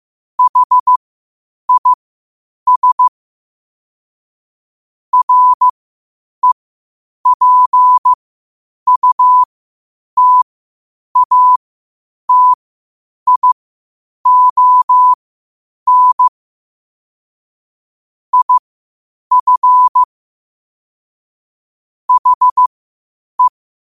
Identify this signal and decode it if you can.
Morse code, heard as a keyed tone, about 15 wpm, spaced as for 8 wpm, 1000 Hz. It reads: HIS REPUTATION IF HE
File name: qotdfiles / 8wpm